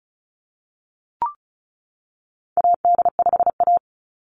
Morse code, heard as a keyed tone, about 34 wpm, 700 Hz. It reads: AD5A
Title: NA SOTA Activator Callsigns 35wpm